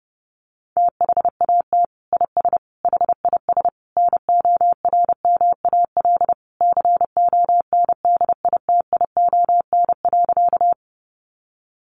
Morse code, heard as a keyed tone, about 30 wpm, 700 Hz. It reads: THAT IS HIS NORMAL CONDITION.
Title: Sentences from Top 1200 Words 30wpm